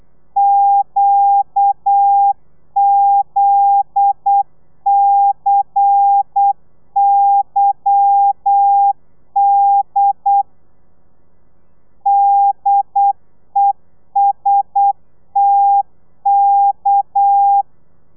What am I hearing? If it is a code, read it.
Morse code, heard as a keyed tone, about 8 wpm, 800 Hz. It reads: QZCYD DESTK
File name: telegrafia / i2tfj / Lez10 35